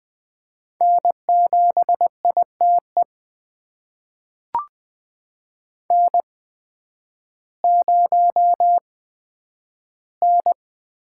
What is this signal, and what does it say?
Morse code, heard as a keyed tone, about 20 wpm, spaced as for 11 wpm, 700 Hz. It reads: N7ITE N0N
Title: USA Call Signs - 5 Characters - 8x Character Spacing - 20wpm